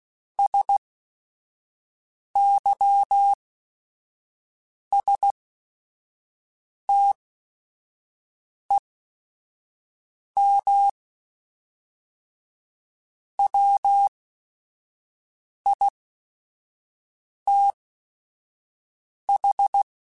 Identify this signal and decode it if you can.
Morse code, heard as a keyed tone, about 16 wpm, spaced as for 5 wpm, 800 Hz. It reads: SYSTEM WITH
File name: Chuckles / Chuckles-mp3s / Guaranteed @5